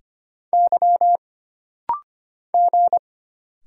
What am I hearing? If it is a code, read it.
Morse code, heard as a keyed tone, about 25 wpm, 700 Hz. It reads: Y G
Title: Single Letter-Number Alt - Rapid-Fire 25wpm